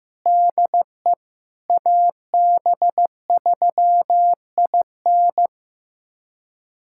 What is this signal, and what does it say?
Morse code, heard as a keyed tone, about 16 wpm, 700 Hz. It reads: DE AB3IN
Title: USA Callsign DE Callsign - 5 Characters - 15wpm